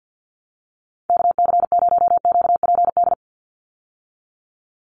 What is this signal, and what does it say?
Morse code, heard as a keyed tone, about 50 wpm, 700 Hz. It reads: KC0QPD
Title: USA Call Signs 50wpm